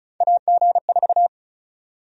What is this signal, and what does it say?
Morse code, heard as a keyed tone, about 36 wpm, 700 Hz. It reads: AG4
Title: USA Call Sign Prefixes - 35wpm